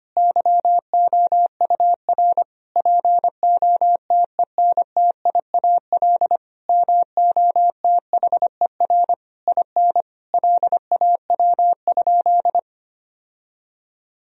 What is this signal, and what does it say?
Morse code, heard as a keyed tone, about 25 wpm, 700 Hz. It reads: YOUR POTENTIAL MOTHER IN LAW?
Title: Sentences from Top 1000 Words - Review Part 4 - 25wpm